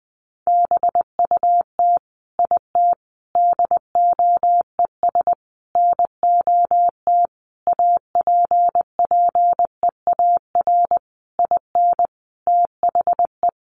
Morse code, heard as a keyed tone, about 20 wpm, 700 Hz. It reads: BUT IT DOES NOT APPEAR IN THE
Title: Sentences from Top 1200 Words 20wpm